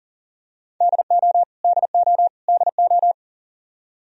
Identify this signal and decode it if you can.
Morse code, heard as a keyed tone, about 37 wpm, 700 Hz. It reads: DO DO DO
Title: Top 100 Words Repeated 40wpm